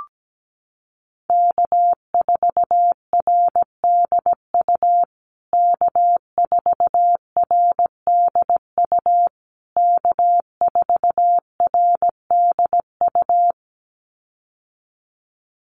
Morse code, heard as a keyed tone, about 17 wpm, 700 Hz. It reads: K4RDU K4RDU K4RDU
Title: USA Call Signs Repeated 3x 17wpm